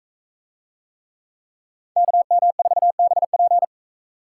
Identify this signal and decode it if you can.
Morse code, heard as a keyed tone, about 39 wpm, 700 Hz. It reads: KM4BP